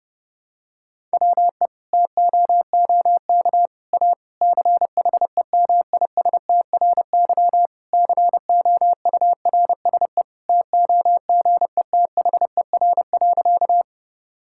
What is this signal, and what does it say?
Morse code, heard as a keyed tone, about 30 wpm, 700 Hz. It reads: WE TOOK A CHEMISTRY COURSE TOGETHER.
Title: Sentences from Top 1000 Words - Review Part 6 - 30wpm